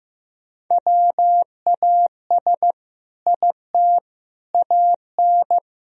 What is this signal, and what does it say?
Morse code, heard as a keyed tone, about 15 wpm, 700 Hz. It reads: WAS IT AN
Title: Sentences from Top 1000 Words - Review Part 4 - 15wpm